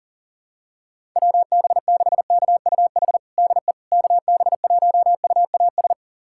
Morse code, heard as a keyed tone, about 40 wpm, 700 Hz. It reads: WB6KUH DE KB1UAS
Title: USA Callsign DE Callsign - 6 Characters - 40wpm